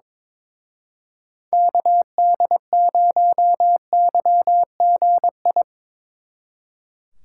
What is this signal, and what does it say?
Morse code, heard as a keyed tone, about 24 wpm, 700 Hz. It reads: KD0YGI
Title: Speed Racing - USA 2x3 Call Signs 22wpm